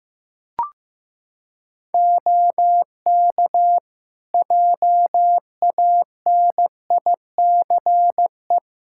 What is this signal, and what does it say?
Morse code, heard as a keyed tone, about 15 wpm, 700 Hz. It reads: OK JANICE
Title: Practice QSO - Exchange 3 - More Info - 15wpm